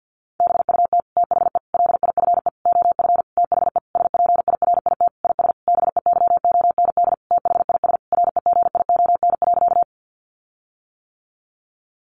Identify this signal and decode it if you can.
Morse code, heard as a keyed tone, about 50 wpm, 700 Hz. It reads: BUT THE LIFE OF THE SPIRIT IS BEYOND THIS REGION.
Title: Sentences from Top 1500 Words 50wpm